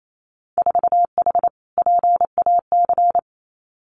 Morse code, heard as a keyed tone, about 28 wpm, 690 Hz. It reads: 4H PAC
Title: Winter Field Day Exchanges Encore 28wpm